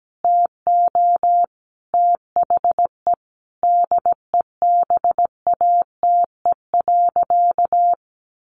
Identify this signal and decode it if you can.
Morse code, heard as a keyed tone, about 17 wpm, 700 Hz. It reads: TO THE DEBATE.
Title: Sentences from Top 1100 Words 17wpm